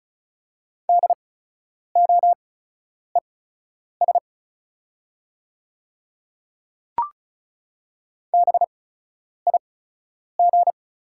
Morse code, heard as a keyed tone, about 36 wpm, spaced as for 10 wpm, 700 Hz. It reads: DOES BIG
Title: 8x Character Spacing - Top 200 Words - 35wpm